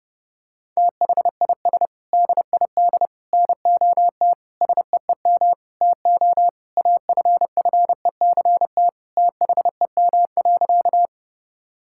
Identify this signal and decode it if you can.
Morse code, heard as a keyed tone, about 30 wpm, 700 Hz. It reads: THIS DID NOT SEEM TO AFFECT THEM.
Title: Sentences from Top 1000 Words - Review Part 6 - 30wpm